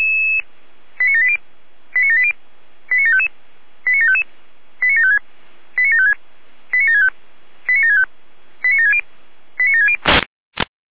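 Five-tone selective call (ZVEI), heard as quick runs of five tones, tones of about 70 ms. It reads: 89780 89780 89755 89755 89756 89756 89765 89765 89760 89760
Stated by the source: ZVEI five-tone selective call received by SDR.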